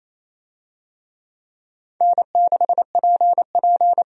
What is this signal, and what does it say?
Morse code, heard as a keyed tone, about 28 wpm, 700 Hz. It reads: N6PP
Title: USA Call Signs - 4 Characters - 28wpm